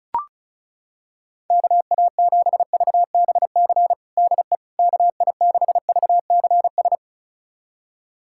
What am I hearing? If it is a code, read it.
Morse code, heard as a keyed tone, about 34 wpm, 700 Hz. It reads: KA7VBC DE KI6VCS
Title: USA Callsign DE Callsign - 6 Characters - 35wpm